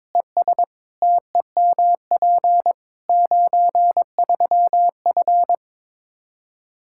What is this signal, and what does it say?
Morse code, heard as a keyed tone, about 21 wpm, 700 Hz. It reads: ES TEMP 93F
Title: K1USN SST QSO Practice 22wpm